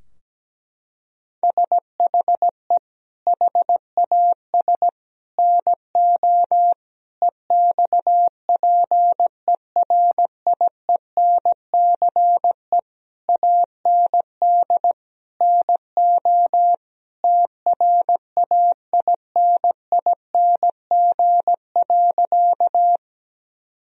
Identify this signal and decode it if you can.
Morse code, heard as a keyed tone, about 17 wpm, 700 Hz. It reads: SHE HAS NO EXPERIENCE AND NO TRAINING.
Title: Sentences from Top 1000 Words - Review Part 4 - 17wpm